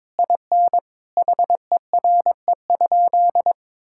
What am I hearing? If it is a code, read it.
Morse code, heard as a keyed tone, about 23 wpm, 700 Hz. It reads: IN HERE?
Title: Sentences from Top 1000 Words - Review Part 6 - 22wpm